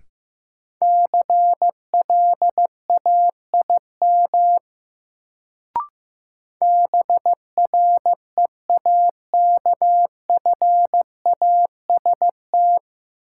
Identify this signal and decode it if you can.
Morse code, heard as a keyed tone, about 15 wpm, 700 Hz. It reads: CLAIM BREAKFAST
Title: Sentences from Top 1500 Words - New Word Drill Alt - 15wpm